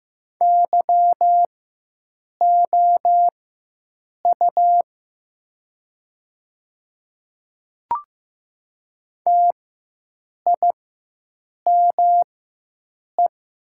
Morse code, heard as a keyed tone, about 15 wpm, spaced as for 7 wpm, 700 Hz. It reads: YOU TIME
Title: 4x Character Spacing - Top 300 Words - 15wpm